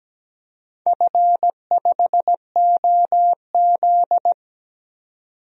Morse code, heard as a keyed tone, about 17 wpm, 700 Hz. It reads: F5OZ
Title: DXCC-100 Call Signs - 4 Characters - 17wpm